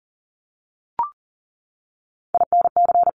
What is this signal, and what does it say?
Morse code, heard as a keyed tone, about 40 wpm, 700 Hz. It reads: INC
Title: Top 1751-2000 Words 40wpm